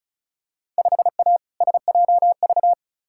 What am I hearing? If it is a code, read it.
Morse code, heard as a keyed tone, about 36 wpm, 700 Hz. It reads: 5A SJV
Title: Field Day Exchanges - 35wpm